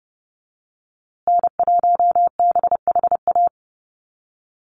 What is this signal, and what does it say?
Morse code, heard as a keyed tone, about 30 wpm, 700 Hz. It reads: N1BHA
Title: USA Call Signs Repeated 3x 30wpm